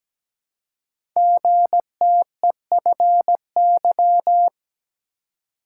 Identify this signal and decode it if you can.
Morse code, heard as a keyed tone, about 17 wpm, 700 Hz. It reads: GTEFY
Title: Code Groups - 5 Letter:Numbers - 17wpm